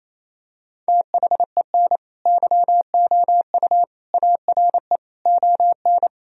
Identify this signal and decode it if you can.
Morse code, heard as a keyed tone, about 28 wpm, 700 Hz. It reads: THEN YOU ARE ON
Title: Sentences from Top 1000 Words - Review Part 4 - 28wpm